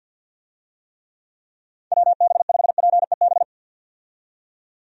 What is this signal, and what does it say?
Morse code, heard as a keyed tone, about 50 wpm, 700 Hz. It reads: WB5PEB
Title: USA Call Signs - 6 Characters - 50wpm